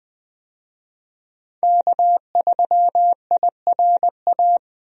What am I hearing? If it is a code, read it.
Morse code, heard as a keyed tone, about 20 wpm, 700 Hz. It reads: K3IRA
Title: USA Call Signs - 5 Characters - 20wpm